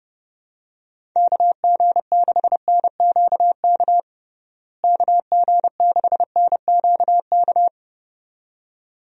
Morse code, heard as a keyed tone, about 30 wpm, 700 Hz. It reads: KG6NQK KG6NQK
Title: USA Call Signs - 6 Characters - Repeated 2x - 30wpm